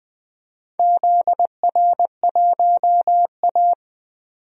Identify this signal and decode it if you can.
Morse code, heard as a keyed tone, about 20 wpm, 700 Hz. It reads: ZR1A
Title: DXCC-100 Call Signs - 4 Characters - 20wpm